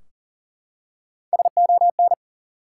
Morse code, heard as a keyed tone, about 40 wpm, 700 Hz. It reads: SON
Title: Three Letter Words - 40wpm